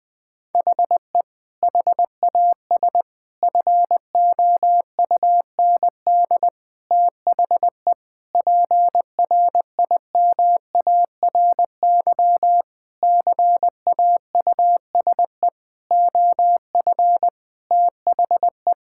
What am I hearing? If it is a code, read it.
Morse code, heard as a keyed tone, about 20 wpm, 700 Hz. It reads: HE HAS FOUND THE PRIMARY CAUSE OF THE
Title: Sentences from Top 1000 Words - Review Part 5 - 20wpm